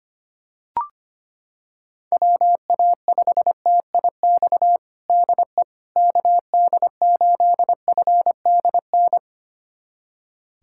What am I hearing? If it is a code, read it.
Morse code, heard as a keyed tone, about 25 wpm, 700 Hz. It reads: WA5TIX DE KD8FDN
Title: USA Callsign DE Callsign - 6 Characters - 25wpm